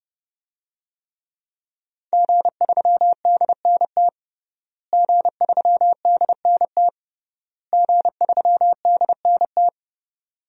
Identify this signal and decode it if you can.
Morse code, heard as a keyed tone, about 32 wpm, 700 Hz. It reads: G3DNT G3DNT G3DNT
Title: DXCC-100 Call Signs - 5 Characters - Repeated 3x - 30wpm